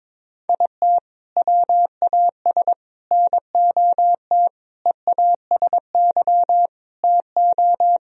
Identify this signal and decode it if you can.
Morse code, heard as a keyed tone, about 22 wpm, 700 Hz. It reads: IT WAS NOT EASY TO
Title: Sentences from Top 1000 Words - Review Part 2 - 22wpm